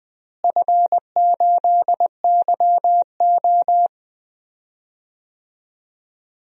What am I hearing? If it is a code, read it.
Morse code, heard as a keyed tone, about 20 wpm, 700 Hz. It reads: F8YO
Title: DXCC-100 Call Signs - 4 Characters - Repeated 3x - 20wpm